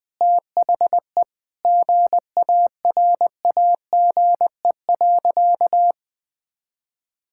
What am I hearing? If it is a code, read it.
Morse code, heard as a keyed tone, about 20 wpm, 700 Hz. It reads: THE GARAGE.
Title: Sentences from Top 1600 Words 20wpm